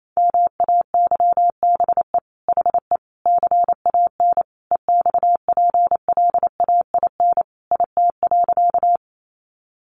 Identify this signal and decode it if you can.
Morse code, heard as a keyed tone, about 28 wpm, 700 Hz. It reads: MAYBE HE CAN EXPLAIN IT.
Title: Sentences from Top 1100 Words 28wpm